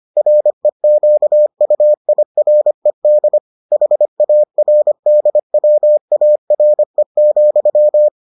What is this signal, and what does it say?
Morse code, heard as a keyed tone, about 25 wpm, 600 Hz. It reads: REQUIRED HARDWARE,